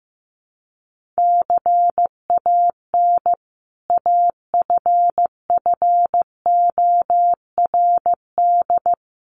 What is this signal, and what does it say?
Morse code, heard as a keyed tone, about 15 wpm, 700 Hz. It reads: CAN AFFORD
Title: Sentences from Top 1100 Words 15wpm